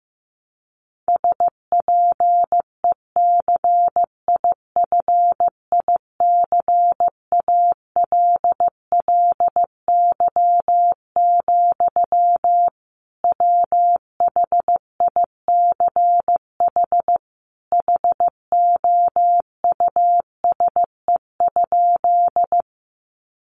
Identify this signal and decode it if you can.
Morse code, heard as a keyed tone, about 15 wpm, 700 Hz. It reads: SPECIFICALLY, WHICH HOUSE?
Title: Sentences from Top 1100 Words 15wpm